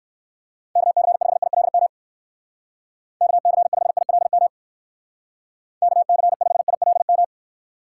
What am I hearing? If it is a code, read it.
Morse code, heard as a keyed tone, about 57 wpm, 700 Hz. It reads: KC5ILM KC5ILM KC5ILM